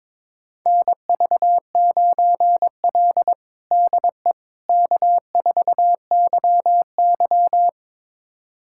Morse code, heard as a keyed tone, about 22 wpm, 700 Hz. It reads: NV9L DE K4YY
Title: USA Callsign DE Callsign - 4 Characters - 22wpm